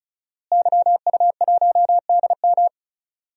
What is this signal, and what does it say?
Morse code, heard as a keyed tone, about 36 wpm, 700 Hz. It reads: YU1DM
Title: DXCC-100 Call Signs - 5 Characters - Repeated 3x - 35wpm